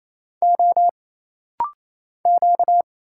Morse code, heard as a keyed tone, about 28 wpm, 700 Hz. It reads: O Q
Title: Single Letter-Number Alt - Rapid-Fire 28wpm